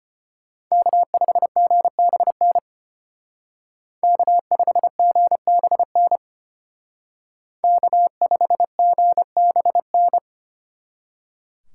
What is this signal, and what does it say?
Morse code, heard as a keyed tone, about 31 wpm, 700 Hz. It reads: K5GBN K5GBN K5GBN